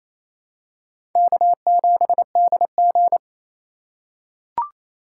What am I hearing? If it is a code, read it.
Morse code, heard as a keyed tone, about 28 wpm, 700 Hz. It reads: K7DG
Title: USA Call Signs - 3x Character Spacing - 28wpm